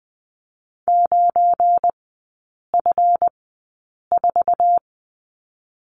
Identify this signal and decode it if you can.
Morse code, heard as a keyed tone, about 20 wpm, 700 Hz. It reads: 9 F 4
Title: 184 Lesson 30e - Letter b - Three Characters 20wpm